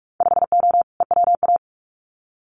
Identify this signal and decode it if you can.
Morse code, heard as a keyed tone, about 44 wpm, 700 Hz. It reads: 5O EWA
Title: Winter Field Day Exchanges Encore 45wpm